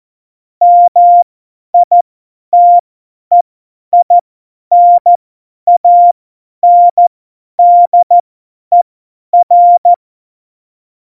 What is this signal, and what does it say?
Morse code, heard as a keyed tone, about 14 wpm, spaced as for 10 wpm, 700 Hz. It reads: MITEINANDER
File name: Morsekurs / Witze / Witz 24. November 2020, 50 F70